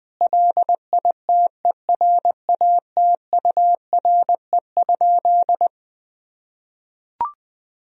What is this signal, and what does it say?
Morse code, heard as a keyed tone, about 20 wpm, 700 Hz. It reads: LITERATURE?
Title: Sentences from Top 1000 Words - Review Part 3 - 20wpm